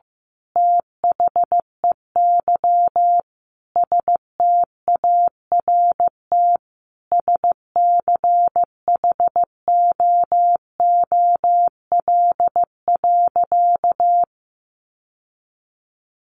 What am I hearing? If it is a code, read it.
Morse code, heard as a keyed tone, about 15 wpm, 700 Hz. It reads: THEY START SCHOOL.
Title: Sentences from Top 1100 Words 15wpm